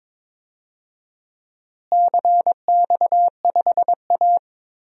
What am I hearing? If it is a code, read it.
Morse code, heard as a keyed tone, about 21 wpm, 700 Hz. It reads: CX5A
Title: DXCC-100 Call Signs - 4 Characters - Repeated 3x - 22wpm